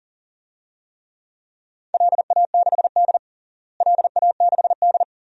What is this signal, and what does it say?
Morse code, heard as a keyed tone, about 40 wpm, 700 Hz. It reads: LA6D LA6D
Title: DXCC-100 Call Signs - 4 Characters - Repeated 2x - 40wpm